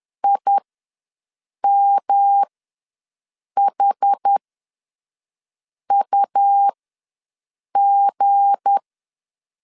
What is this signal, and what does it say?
Morse code, heard as a keyed tone, about 10 wpm, spaced as for 5 wpm, 800 Hz. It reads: IMHUG
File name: LEZIONI / 9 2 ws 10wpm 4130